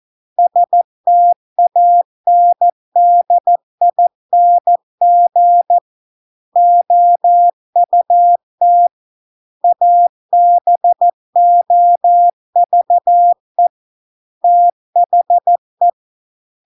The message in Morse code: STANDING OUT ABOVE THE

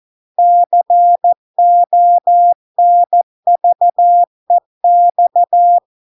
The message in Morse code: CONVEX

